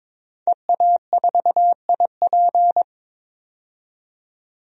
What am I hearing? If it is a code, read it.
Morse code, heard as a keyed tone, about 22 wpm, 700 Hz. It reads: EA4IP
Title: DXCC-100 Call Signs - 5 Characters - 22wpm